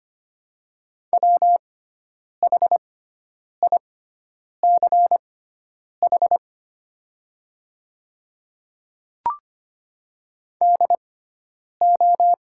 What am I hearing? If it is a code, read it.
Morse code, heard as a keyed tone, about 25 wpm, spaced as for 9 wpm, 700 Hz. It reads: WHICH DO
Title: 6x Character Spacing - Top 100 Words - 25wpm